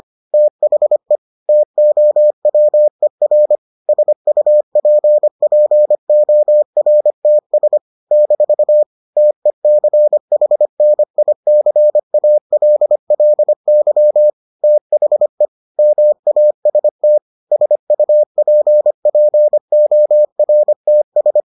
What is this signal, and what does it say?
Morse code, heard as a keyed tone, about 25 wpm, 600 Hz. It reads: THE TOWER SUPPORTS - TECHNICALLY THE MAST SUPPORTS